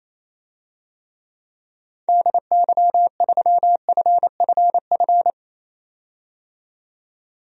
Morse code, heard as a keyed tone, about 27 wpm, 700 Hz. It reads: DY3FFF
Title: DXCC-100 Call Signs - 6 Characters - 28wpm